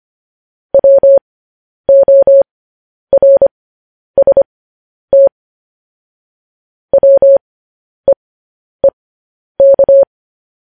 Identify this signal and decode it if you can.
Morse code, heard as a keyed tone, about 25 wpm, spaced as for 10 wpm, 550 Hz. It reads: WORST WEEK
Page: news headlines in morse code at 10 wpm